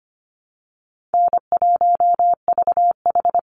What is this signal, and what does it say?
Morse code, heard as a keyed tone, about 25 wpm, 700 Hz. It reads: N1VH
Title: Call Signs Repeated 3x - 25wpm